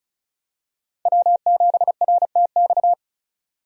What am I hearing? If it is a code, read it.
Morse code, heard as a keyed tone, about 34 wpm, 700 Hz. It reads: W7RTX
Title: USA Call Signs - 5 Characters - 35wpm